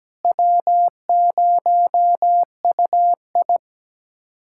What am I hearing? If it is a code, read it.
Morse code, heard as a keyed tone, about 17 wpm, 700 Hz. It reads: W0UI
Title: USA Callsign DE Callsign - 4 Characters - 17wpm